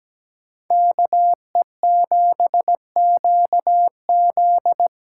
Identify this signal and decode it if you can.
Morse code, heard as a keyed tone, about 17 wpm, 700 Hz. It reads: K E 7 Q Z
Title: USA Call Signs - 6 Characters - 8x Character Spacing - 17wpm